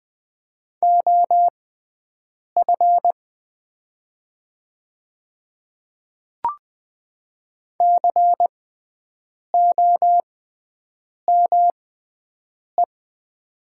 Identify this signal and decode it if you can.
Morse code, heard as a keyed tone, about 20 wpm, spaced as for 7 wpm, 700 Hz. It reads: OF COME
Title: 6x Character Spacing - Top 100 Words - 20wpm